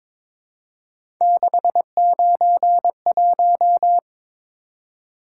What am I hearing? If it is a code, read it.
Morse code, heard as a keyed tone, about 22 wpm, 700 Hz. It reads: 691
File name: practice / 22wpm / Three Letter-Number Combinations - 22wpm